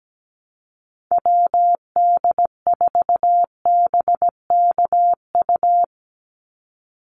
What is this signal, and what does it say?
Morse code, heard as a keyed tone, about 17 wpm, 700 Hz. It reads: WD4BKU
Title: USA Call Signs 17wpm